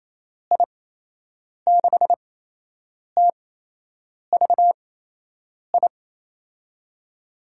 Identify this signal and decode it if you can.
Morse code, heard as a keyed tone, about 28 wpm, spaced as for 8 wpm, 700 Hz. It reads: I6TVI
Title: USA Call Signs - 8x Character Spacing - 28wpm